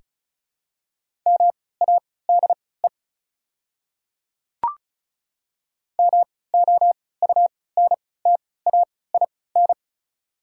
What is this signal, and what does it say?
Morse code, heard as a keyed tone, about 35 wpm, spaced as for 20 wpm, 690 Hz. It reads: MADE MOUNTAIN
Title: 3x Character Spacing - Top 300 Words - 35wpm